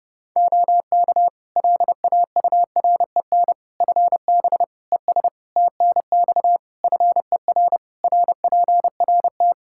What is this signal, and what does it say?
Morse code, heard as a keyed tone, about 30 wpm, 700 Hz. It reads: OK LAUREN FB ES TNX FER RPRT
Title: Practice QSO - Exchange 2 - Info - 30wpm